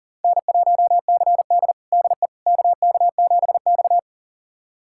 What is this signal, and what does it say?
Morse code, heard as a keyed tone, about 40 wpm, 700 Hz. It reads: N1CD DE KK7X
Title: USA Callsign DE Callsign - 4 Characters - 40wpm